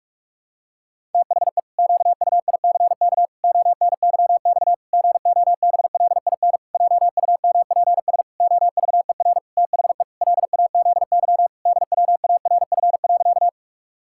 Morse code, heard as a keyed tone, about 46 wpm, 700 Hz. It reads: THE QUICK ONYX GOBLIN JUMPS OVER THE LAZY DWARF.
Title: Pangram Challenge v2 - 45wpm